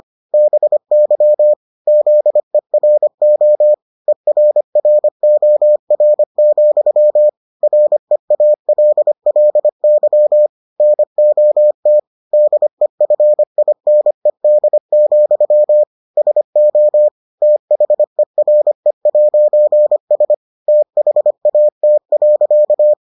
Morse code, heard as a keyed tone, about 25 wpm, 600 Hz. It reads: BY ZERO ERROR, REALLY NOT DEFINED, SO THERE'S THAT.